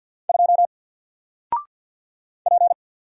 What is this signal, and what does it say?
Morse code, heard as a keyed tone, about 49 wpm, 700 Hz. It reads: 2 P
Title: Single Character (Letter:Number:Punctuation:Prosign) Alt 50wpm